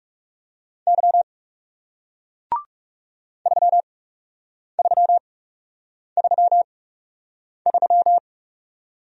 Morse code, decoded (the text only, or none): Y 3333